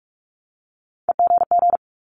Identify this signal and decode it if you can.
Morse code, heard as a keyed tone, about 46 wpm, 700 Hz. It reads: EGG
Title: Multi-letter Phonemes 45wpm